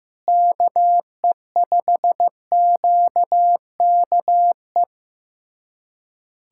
Morse code, heard as a keyed tone, about 15 wpm, 700 Hz. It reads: KE5QKE
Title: USA Callsign DE Callsign - 6 Characters - 15wpm